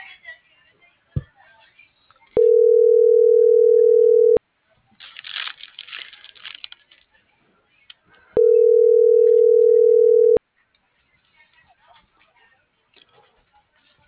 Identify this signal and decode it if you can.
Ringback tone (North America), 440 + 480 Hz.